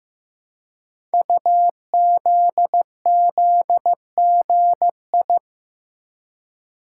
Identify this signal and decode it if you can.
Morse code, heard as a keyed tone, about 15 wpm, 700 Hz. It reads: UZZGI
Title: Code Groups - 5 Letters - 15wpm